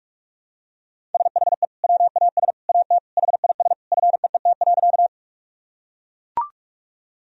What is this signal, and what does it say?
Morse code, heard as a keyed tone, about 41 wpm, 700 Hz. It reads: SHE WAS AT HIS FEET.
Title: Sentences from Top 1000 Words - Review Part 1 - 45wpm